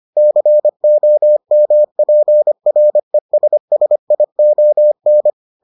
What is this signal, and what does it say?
Morse code, heard as a keyed tone, about 25 wpm, 600 Hz. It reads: COMPRESSION